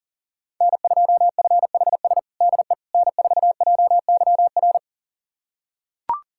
Morse code, heard as a keyed tone, about 40 wpm, 700 Hz. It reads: N2FHS DE N4JYR
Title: USA Callsign DE Callsign - 5 Characters - 40wpm